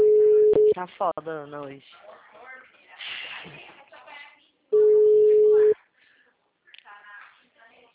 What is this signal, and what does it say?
Ringback tone (Europe, 425 Hz), with speech.